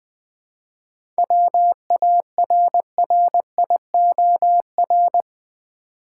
Morse code, heard as a keyed tone, about 21 wpm, 700 Hz. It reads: WARRIOR
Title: 7 Letter Words - First 3 Letters Given - 20wpm